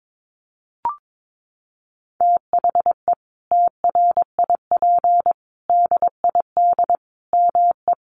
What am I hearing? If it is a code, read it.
Morse code, heard as a keyed tone, about 22 wpm, 700 Hz. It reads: THE TRIP DID ME
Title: Sentences from Top 1300 Words 22wpm